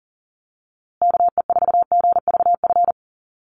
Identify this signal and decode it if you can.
Morse code, heard as a keyed tone, about 40 wpm, 700 Hz. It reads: KE4GVF
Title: USA Call Signs Repeated 3x 40wpm